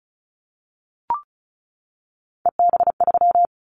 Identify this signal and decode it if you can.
Morse code, heard as a keyed tone, about 34 wpm, 700 Hz. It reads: EB3
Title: Callsign Prefixes for Top 25 DXCC - 35wpm